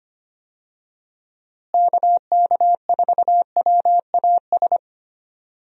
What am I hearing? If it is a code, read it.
Morse code, heard as a keyed tone, about 25 wpm, 700 Hz. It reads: KK4WAS
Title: USA Call Signs - 6 Characters - Repeated 3x - 25wpm